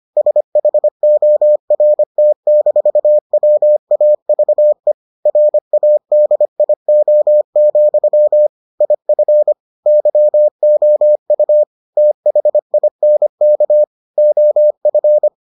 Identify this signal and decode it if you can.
Morse code, heard as a keyed tone, about 25 wpm, 600 Hz. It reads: SHORT-WAVE RADIO, IF YOU THINK OF